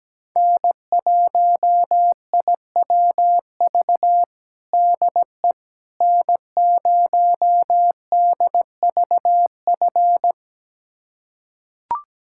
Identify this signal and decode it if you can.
Morse code, heard as a keyed tone, about 17 wpm, 700 Hz. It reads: N1IWV DE N0DVF
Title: USA Callsign DE Callsign - 5 Characters - 17wpm